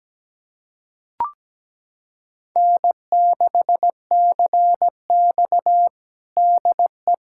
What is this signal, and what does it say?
Morse code, heard as a keyed tone, about 17 wpm, 700 Hz. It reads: N6CX DE
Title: USA Callsign DE Callsign - 4 Characters - 17wpm